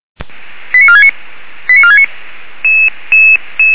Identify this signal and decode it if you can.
Five-tone selective call (ZVEI), heard as quick runs of five tones, tones of about 80 ms. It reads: 89469 89469